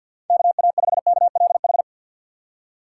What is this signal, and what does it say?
Morse code, heard as a keyed tone, about 49 wpm, 700 Hz. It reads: KA5KLH
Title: USA Call Signs - 6 Characters - 6x Character Spacing - 50wpm